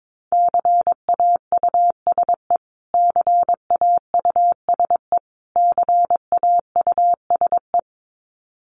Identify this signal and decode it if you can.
Morse code, heard as a keyed tone, about 22 wpm, 700 Hz. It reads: CAUSE CAUSE CAUSE
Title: Top 500 Words Repeated 22wpm